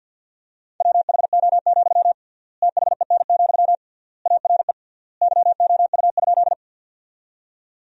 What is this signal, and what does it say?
Morse code, heard as a keyed tone, about 50 wpm, 700 Hz. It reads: WHO, THEN, ARE YOU?
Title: 3x Word-Spacing - Sentences from Top 100 Words - 50wpm